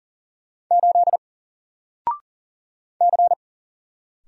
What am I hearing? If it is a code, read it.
Morse code, heard as a keyed tone, about 40 wpm, 700 Hz. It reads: 8 C
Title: Single Character (Letter:Number:Punctuation:Prosign) Alt 40wpm